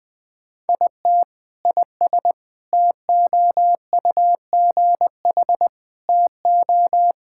Morse code, heard as a keyed tone, about 20 wpm, 700 Hz. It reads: IT IS TOUGH TO
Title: Sentences with 6-Word Phrases - 20wpm